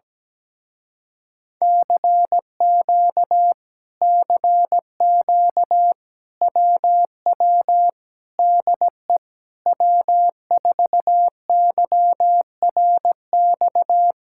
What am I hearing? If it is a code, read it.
Morse code, heard as a keyed tone, about 17 wpm, 700 Hz. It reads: CQ CQ WW DE W4YRX
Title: Speed-Racing Common QSO Phrases Encore - 17wpm